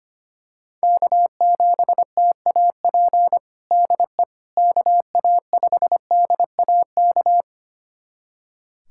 Morse code, heard as a keyed tone, about 25 wpm, 700 Hz. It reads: K7TAP DE KA5DAK
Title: USA Callsign DE Callsign 25wpm